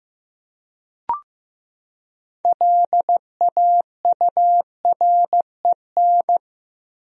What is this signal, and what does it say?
Morse code, heard as a keyed tone, about 16 wpm, 700 Hz. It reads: LAUREN
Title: USA Common Names - 15wpm